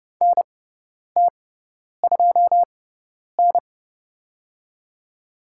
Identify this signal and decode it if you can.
Morse code, heard as a keyed tone, about 30 wpm, spaced as for 10 wpm, 700 Hz. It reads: NT2N
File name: Lesson 16g - Letter c - Call Signs 30wpm